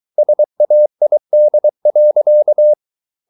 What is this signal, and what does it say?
Morse code, heard as a keyed tone, about 23 wpm, 600 Hz. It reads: SAID.